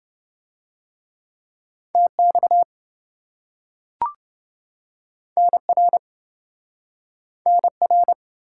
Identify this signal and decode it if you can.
Morse code, heard as a keyed tone, about 28 wpm, 700 Hz. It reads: TX NR NR